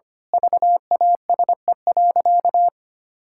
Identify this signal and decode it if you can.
Morse code, heard as a keyed tone, about 25 wpm, 700 Hz. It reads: VASE.
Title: Pangram Challenge v2 - 25wpm